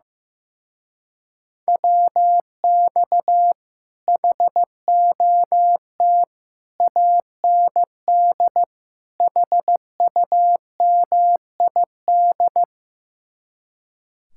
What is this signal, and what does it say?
Morse code, heard as a keyed tone, about 16 wpm, 700 Hz. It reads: WX HOT AND HUMID
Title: Speed-Racing Common QSO Phrases Encore - 15wpm